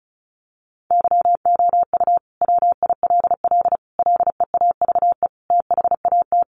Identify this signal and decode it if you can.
Morse code, heard as a keyed tone, about 34 wpm, 700 Hz. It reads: YOU WILL LEAVE THAT
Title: Sentences from Top 900 Words 35wpm